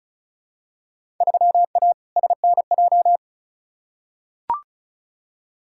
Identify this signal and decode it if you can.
Morse code, heard as a keyed tone, about 36 wpm, 700 Hz. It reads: 3A SNJ
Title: Field Day Exchanges - 35wpm